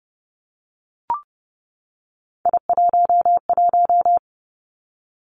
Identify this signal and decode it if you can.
Morse code, heard as a keyed tone, about 32 wpm, 700 Hz. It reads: I11
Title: Three Challenging Letter-Number Combinations 30wpm